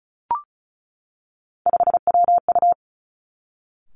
Morse code, heard as a keyed tone, about 36 wpm, 700 Hz. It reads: 5WU
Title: Three Challenging Letter-Number Combinations 35wpm